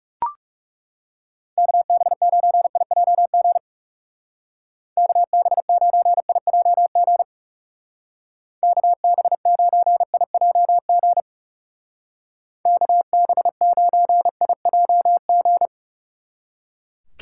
Morse code, heard as a keyed tone, about 37 wpm, 700 Hz. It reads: KB9IJG KB9IJG KB9IJG KB9IJG
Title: Speed Racing - USA 2x3 Call Signs 30wpm